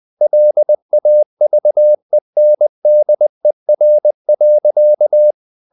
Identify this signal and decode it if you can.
Morse code, heard as a keyed tone, about 20 wpm, 600 Hz. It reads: LAVENDER.
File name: the old man and the sea / the old man and the sea part 1 20wpm